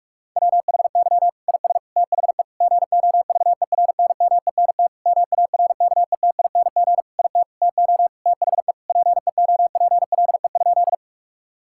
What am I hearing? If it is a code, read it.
Morse code, heard as a keyed tone, about 46 wpm, 700 Hz. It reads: WHY IS THE GOVERNMENT MARKETING IT TO THE PEOPLE?
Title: Sentences from Top 1000 Words - Review Part 5 - 45wpm